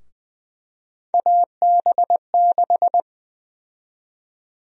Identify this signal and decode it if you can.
Morse code, heard as a keyed tone, about 20 wpm, 700 Hz. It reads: AB6
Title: USA Call Sign Prefixes - Repeated 2x - 20wpm